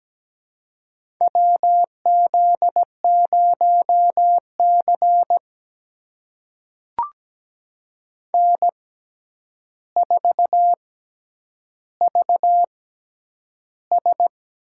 Morse code, heard as a keyed tone, about 17 wpm, spaced as for 9 wpm, 700 Hz. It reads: WZ0C N4VS
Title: USA Call Signs - 4 Characters - 6x Character Spacing - 17wpm